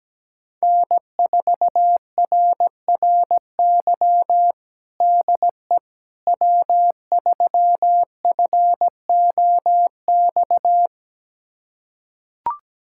Morse code, heard as a keyed tone, about 17 wpm, 700 Hz. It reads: N4RRY DE W3FOX
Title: USA Callsign DE Callsign - 5 Characters - 17wpm